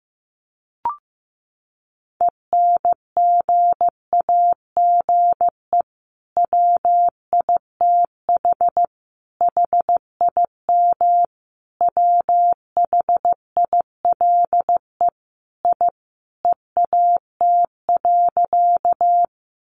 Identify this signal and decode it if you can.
Morse code, heard as a keyed tone, about 14 wpm, 700 Hz. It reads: ENGAGE WITH HIM WHILE I EAT.
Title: Sentences from Top 1500 Words 15wpm